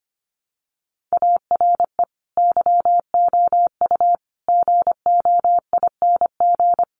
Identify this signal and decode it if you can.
Morse code, heard as a keyed tone, about 25 wpm, 700 Hz. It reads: ARE YOU GOING
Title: Sets of 5 Words 25wpm